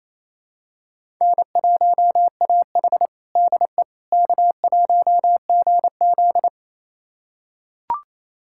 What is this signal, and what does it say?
Morse code, heard as a keyed tone, about 28 wpm, 700 Hz. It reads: N1AH DE K1GZ
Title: USA Callsign DE Callsign - 4 Characters - 28wpm